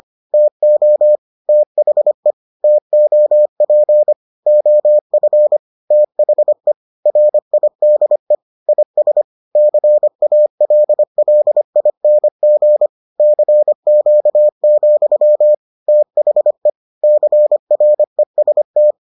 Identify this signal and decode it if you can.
Morse code, heard as a keyed tone, about 25 wpm, 600 Hz. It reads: TO THE TOP OF THE RIDE IS CALLING CQ, THE CREST